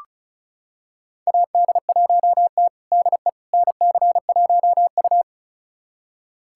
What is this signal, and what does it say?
Morse code, heard as a keyed tone, about 36 wpm, 700 Hz. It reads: AD1T DE NC1U
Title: USA Callsign DE Callsign - 4 Characters - 35wpm